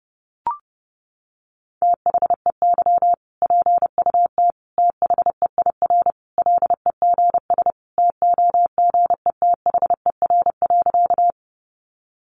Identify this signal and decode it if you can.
Morse code, heard as a keyed tone, about 30 wpm, 700 Hz. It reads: THEY PUT THEIR LEGS TOGETHER.
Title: Easy Phrases 30wpm